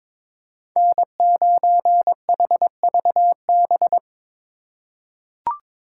Morse code, heard as a keyed tone, about 22 wpm, 700 Hz. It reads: N9HVB
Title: USA Call Signs - 3x Character Spacing - 22wpm